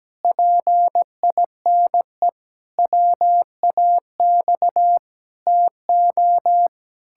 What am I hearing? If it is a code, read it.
Morse code, heard as a keyed tone, about 17 wpm, 700 Hz. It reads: PINE WAX TO
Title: Pangram Challenge v2 - 17wpm